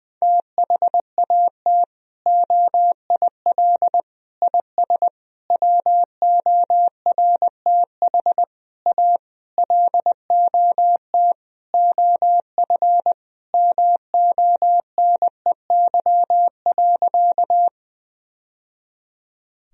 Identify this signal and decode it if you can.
Morse code, heard as a keyed tone, about 20 wpm, 700 Hz. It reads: THAT OIL IS WORTH A LOT OF MONEY.
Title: Sentences from Top 1000 Words - Review Part 4 - 20wpm